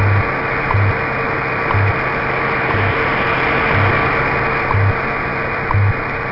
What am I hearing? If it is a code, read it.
Radio time signal, with noise.
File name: wwv / WWV kph 25000kHz amn